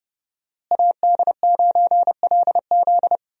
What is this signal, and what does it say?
Morse code, heard as a keyed tone, about 32 wpm, 700 Hz. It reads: AD9LZ
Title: USA Call Signs - 5 Characters - Repeated 3x - 30wpm